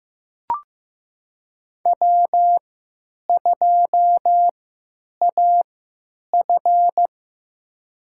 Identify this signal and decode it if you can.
Morse code, heard as a keyed tone, about 15 wpm, spaced as for 9 wpm, 700 Hz. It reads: W2AF
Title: USA Call Signs - 4 Characters - 3x Character Spacing - 15wpm